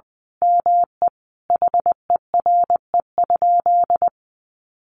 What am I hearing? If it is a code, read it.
Morse code, heard as a keyed tone, about 20 wpm, 700 Hz. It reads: ME HERE?
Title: Sentences from Top 1200 Words 20wpm